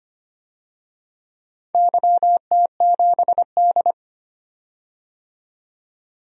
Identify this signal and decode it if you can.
Morse code, heard as a keyed tone, about 25 wpm, 700 Hz. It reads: YT7D
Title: DXCC-100 Call Signs - 4 Characters - 25wpm